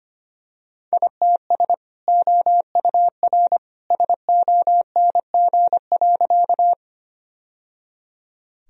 Morse code, heard as a keyed tone, about 25 wpm, 700 Hz. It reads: ITS OUR SONG.
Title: Sentences from Top 1000 Words - Review Part 2 - 25wpm